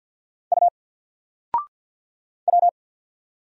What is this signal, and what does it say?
Morse code, heard as a keyed tone, about 54 wpm, 700 Hz. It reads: U W
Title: Single Letter-Number Alt 50wpm